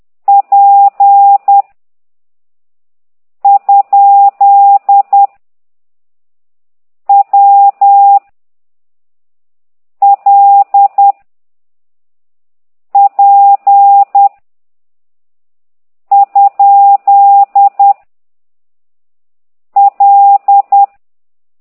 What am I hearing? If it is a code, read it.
Morse code, heard as a keyed tone, about 10 wpm, 800 Hz. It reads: P ? W L P ? L
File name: ct2idw / exerc15